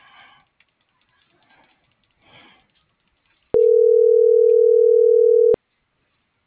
Ringback tone (North America), 440 + 480 Hz.
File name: CENTRAL MULTIMODAL / out josimar.cruz